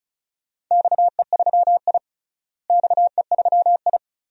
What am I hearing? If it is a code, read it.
Morse code, heard as a keyed tone, about 36 wpm, 700 Hz. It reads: XE3I XE3I
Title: DXCC-100 Call Signs - 4 Characters - Repeated 2x - 35wpm